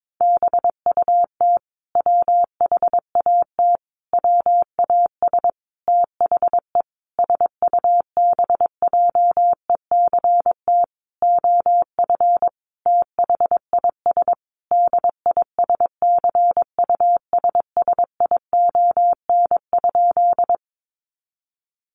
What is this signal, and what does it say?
Morse code, heard as a keyed tone, about 22 wpm, 700 Hz. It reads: BUT WHAT WAS THE SUBJECT OF THIS DISCUSSION?
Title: Sentences from Top 1100 Words 22wpm